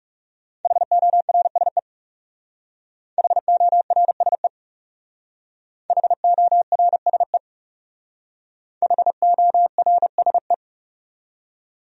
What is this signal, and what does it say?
Morse code, heard as a keyed tone, about 37 wpm, 700 Hz. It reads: HORSE HORSE HORSE HORSE
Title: Speed-Racing Top 500 Words 30wpm